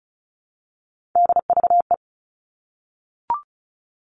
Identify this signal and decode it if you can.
Morse code, heard as a keyed tone, about 34 wpm, 700 Hz. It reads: DVE
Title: All Cut Numbers - 3-Digit 35wpm